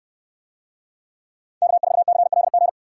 Morse code, heard as a keyed tone, about 65 wpm, 700 Hz. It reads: K4CLG